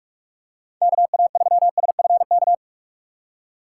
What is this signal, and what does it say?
Morse code, heard as a keyed tone, about 44 wpm, 700 Hz. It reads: KA3SFK
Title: USA Call Signs - 4x Character Spacing - 45wpm